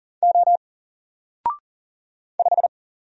Morse code, heard as a keyed tone, about 40 wpm, 700 Hz. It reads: O 5
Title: Single Letter-Number Alt 40wpm